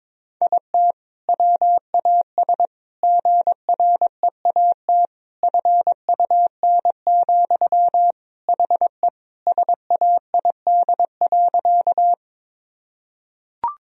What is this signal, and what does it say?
Morse code, heard as a keyed tone, about 22 wpm, 700 Hz. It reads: IT WAS GREAT FUN, HE SAID.
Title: Sentences from Top 1000 Words - Review Part 4 - 22wpm